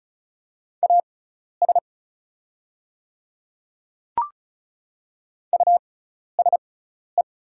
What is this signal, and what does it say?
Morse code, heard as a keyed tone, about 36 wpm, spaced as for 12 wpm, 700 Hz. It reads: AS USE
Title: 6x Character Spacing - Top 200 Words - 35wpm